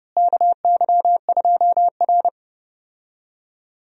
Morse code, heard as a keyed tone, about 30 wpm, 700 Hz. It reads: KY2R
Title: USA Call Signs - 4 Characters - Repeated 3x - 30wpm